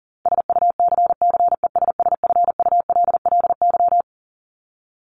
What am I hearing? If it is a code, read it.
Morse code, heard as a keyed tone, about 40 wpm, 700 Hz. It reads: SUCCESSFULLY
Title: Top 1501-1750 Words 40wpm